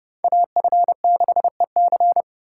Morse code, heard as a keyed tone, about 29 wpm, 700 Hz. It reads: AF6EC